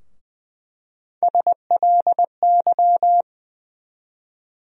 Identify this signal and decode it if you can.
Morse code, heard as a keyed tone, about 20 wpm, 700 Hz. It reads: SLY
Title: USA Call Sign Postfixes - 3 Letters 20wpm